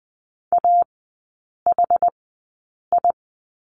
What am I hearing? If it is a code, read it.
Morse code, heard as a keyed tone, about 20 wpm, spaced as for 8 wpm, 690 Hz. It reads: AHI
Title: 060 Lesson 11e - Letter h - Three Characters 20wpm